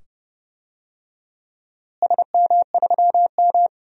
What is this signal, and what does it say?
Morse code, heard as a keyed tone, about 29 wpm, 700 Hz. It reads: SM3M
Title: DXCC-100 Call Signs - 4 Characters - Repeated 2x - 30wpm